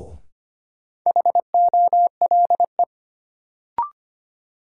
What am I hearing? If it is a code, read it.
Morse code, heard as a keyed tone, about 25 wpm, 700 Hz. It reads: HOLE
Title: Sentences from Top 1600 Words - New Word Drill - 25wpm